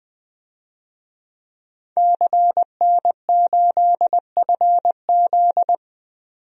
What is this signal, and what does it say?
Morse code, heard as a keyed tone, about 20 wpm, 700 Hz. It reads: CN8FZ
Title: DXCC-100 Call Signs - 5 Characters - 20wpm